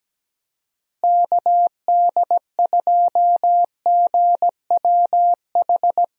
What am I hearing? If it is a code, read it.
Morse code, heard as a keyed tone, about 17 wpm, 700 Hz. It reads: KD2GWH
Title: USA Call Signs - 6 Characters - Repeated 3x - 17wpm